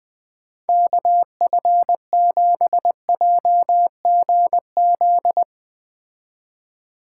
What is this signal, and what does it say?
Morse code, heard as a keyed tone, about 20 wpm, 700 Hz. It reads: KF7JGZ
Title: Speed Racing - USA 2x3 Call Signs 17wpm